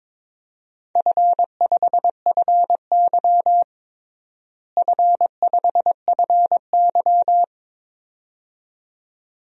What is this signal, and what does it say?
Morse code, heard as a keyed tone, about 22 wpm, 700 Hz. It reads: F5FY F5FY
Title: DXCC-100 Call Signs - 4 Characters - Repeated 2x - 22wpm